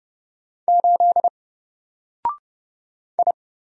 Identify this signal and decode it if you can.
Morse code, heard as a keyed tone, about 30 wpm, 700 Hz. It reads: 8 I
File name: Single-Letter-Number-30wpm